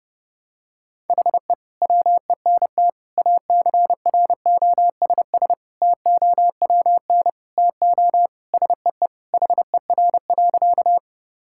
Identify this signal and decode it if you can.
Morse code, heard as a keyed tone, about 30 wpm, 700 Hz. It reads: HE WENT ACROSS TOWN TO SEE HER.
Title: Sentences from Top 1000 Words - Review Part 5 - 30wpm